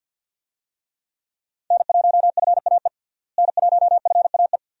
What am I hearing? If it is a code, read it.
Morse code, heard as a keyed tone, about 47 wpm, 700 Hz. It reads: N1FAE N1FAE
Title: USA Call Signs - 5 Characters - Repeated 3x - 50wpm